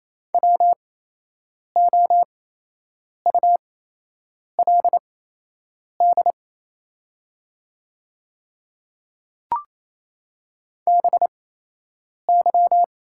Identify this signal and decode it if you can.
Morse code, heard as a keyed tone, about 28 wpm, spaced as for 8 wpm, 700 Hz. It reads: WOULD BY
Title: 8x Character Spacing - Top 100 Words - 28wpm